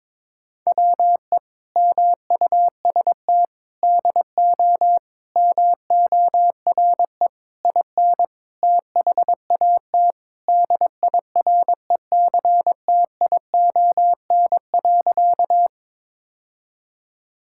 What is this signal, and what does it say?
Morse code, heard as a keyed tone, about 22 wpm, 700 Hz. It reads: WE MUST DO MORE IN THAT DIRECTION.
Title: Sentences from Top 1000 Words - Review Part 5 - 22wpm